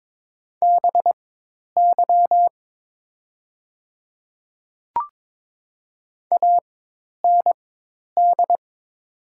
Morse code, heard as a keyed tone, about 22 wpm, spaced as for 10 wpm, 700 Hz. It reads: BY AND
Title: 4x Character Spacing - Top 100 Words - 22wpm